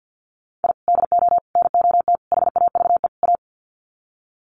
Morse code, heard as a keyed tone, about 50 wpm, 700 Hz. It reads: I DO NOT HAVE A
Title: Sets of 5 Words 50wpm